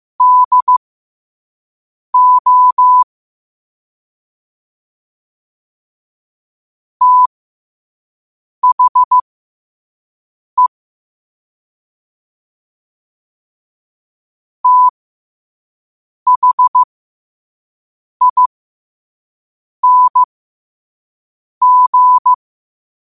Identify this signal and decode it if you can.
Morse code, heard as a keyed tone, about 15 wpm, spaced as for 5 wpm, 1000 Hz. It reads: DO THE THING
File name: qotdfiles / 5wpm